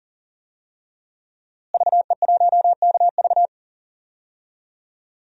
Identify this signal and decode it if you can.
Morse code, heard as a keyed tone, about 40 wpm, 700 Hz. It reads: VE1KV
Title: DXCC-100 Call Signs - 5 Characters - Repeated 2x - 40wpm